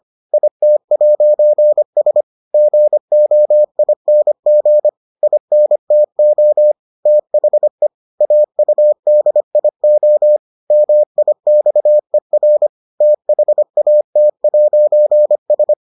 Morse code, heard as a keyed tone, about 25 wpm, 600 Hz. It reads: IT'S GOING INTO THE AUDIO MIXER THAT'S